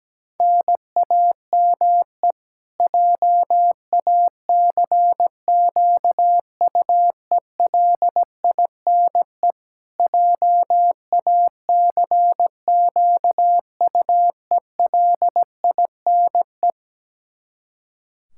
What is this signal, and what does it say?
Morse code, heard as a keyed tone, about 17 wpm, 700 Hz. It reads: NAME JACQUELINE JACQUELINE
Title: Practice QSO - Exchange 1 - Beginning - 17wpm